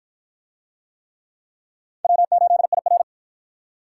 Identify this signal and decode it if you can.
Morse code, heard as a keyed tone, about 47 wpm, 700 Hz. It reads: W8IR